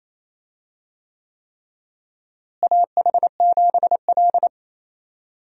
Morse code, heard as a keyed tone, about 28 wpm, 700 Hz. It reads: AH7L
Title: USA Call Signs - 4 Characters - Repeated 3x - 28wpm